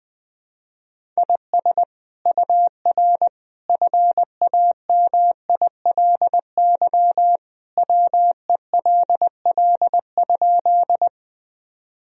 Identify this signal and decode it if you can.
Morse code, heard as a keyed tone, about 21 wpm, 700 Hz. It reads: IS UR FAMILY WELL?
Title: Sentences from Top 400 Words and Abbreviations 20wpm